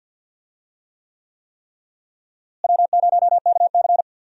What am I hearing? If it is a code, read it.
Morse code, heard as a keyed tone, about 51 wpm, 700 Hz. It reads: W0KC
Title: USA Call Signs - 4 Characters - Repeated 3x - 50wpm